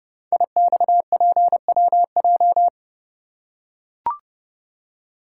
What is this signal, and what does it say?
Morse code, heard as a keyed tone, about 30 wpm, 700 Hz. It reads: IXPWJ
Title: Code Groups - 5 Letters - 30wpm